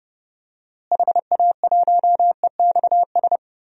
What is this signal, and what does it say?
Morse code, heard as a keyed tone, about 30 wpm, 700 Hz. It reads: HA1EXS
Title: DXCC-100 Call Signs - 6 Characters - Repeated 3x - 30wpm